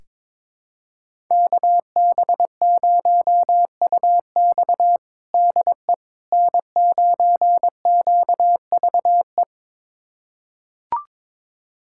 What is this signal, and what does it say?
Morse code, heard as a keyed tone, about 22 wpm, 700 Hz. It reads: KB0UX DE N9QVE
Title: USA Callsign DE Callsign 22wpm